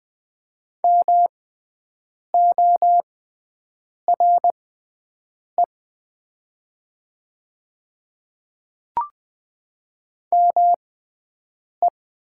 Morse code, heard as a keyed tone, about 21 wpm, spaced as for 7 wpm, 700 Hz. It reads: MORE ME
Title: 6x Character Spacing - Top 200 Words - 20wpm